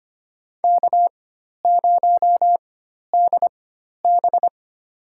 Morse code, heard as a keyed tone, about 25 wpm, spaced as for 12 wpm, 700 Hz. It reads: K0DB
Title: USA Call Signs - 4 Characters - 4x Character Spacing - 25wpm